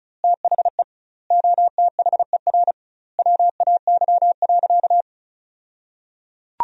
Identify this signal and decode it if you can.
Morse code, heard as a keyed tone, about 36 wpm, 700 Hz. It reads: THE OTHER WAY.
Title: 2x Word-Spacing - Sentences from Top 100 Words - 35wpm